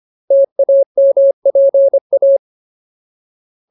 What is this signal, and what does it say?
Morse code, heard as a keyed tone, about 25 wpm, 550 Hz. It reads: TAMPA
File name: Ultimate Word List / 25WPM Ultimate